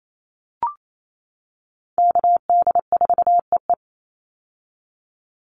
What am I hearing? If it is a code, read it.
Morse code, heard as a keyed tone, about 28 wpm, 700 Hz. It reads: KD4EE
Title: Call Signs - 28wpm